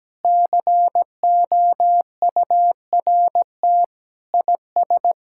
Morse code, heard as a keyed tone, about 17 wpm, 700 Hz. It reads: COURT IS
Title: Sentences from Top 1600 Words 17wpm